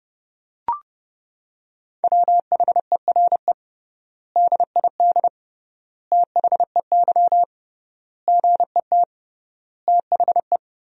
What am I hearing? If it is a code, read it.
Morse code, heard as a keyed tone, about 30 wpm, 700 Hz. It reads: WHERE DID THEY GET THE
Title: 3x Word-Spacing - Sentences from Top 400 Words - 30wpm